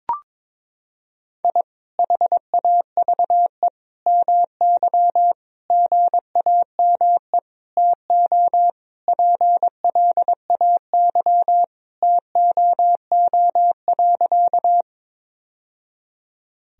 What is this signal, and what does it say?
Morse code, heard as a keyed tone, about 21 wpm, 700 Hz. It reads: I HAVE MY GAME TO PLAY TOO.
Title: Sentences from Top 1000 Words - Review Part 3 - 22wpm